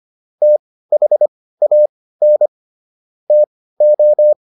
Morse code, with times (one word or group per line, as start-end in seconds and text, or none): 0.4-2.5 THAN
3.3-4.3 TO